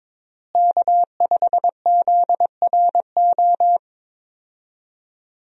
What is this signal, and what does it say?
Morse code, heard as a keyed tone, about 22 wpm, 700 Hz. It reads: K5ZRO
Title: USA Call Signs - 5 Characters - Repeated 3x - 22wpm